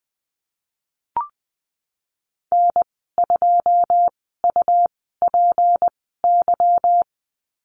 Morse code, heard as a keyed tone, about 20 wpm, spaced as for 15 wpm, 700 Hz. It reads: N2UPY
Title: 161 Lesson 26g - Letter g - Call Signs 20wpm